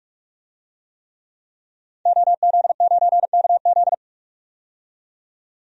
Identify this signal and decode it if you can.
Morse code, heard as a keyed tone, about 42 wpm, 700 Hz. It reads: OZ9KZ
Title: DXCC-100 Call Signs - 5 Characters - 45wpm